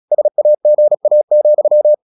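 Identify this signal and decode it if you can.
Morse code, heard as a keyed tone, about 36 wpm, 600 Hz. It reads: SAGA,
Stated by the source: Morse practice at 36 wpm, text of Ken Follett's novel Never, from CW med Gnister.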